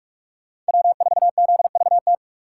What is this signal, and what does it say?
Morse code, heard as a keyed tone, about 57 wpm, 700 Hz. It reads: W4ZVT